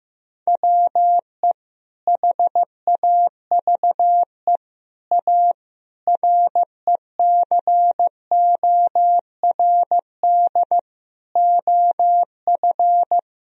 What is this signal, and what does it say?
Morse code, heard as a keyed tone, about 15 wpm, 700 Hz. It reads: WE HAVE A RECORD OF
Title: Sentences from Top 1000 Words - Review Part 3 - 15wpm